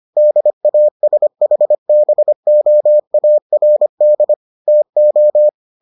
Morse code, heard as a keyed tone, about 25 wpm, 600 Hz. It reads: DASHBOARD TO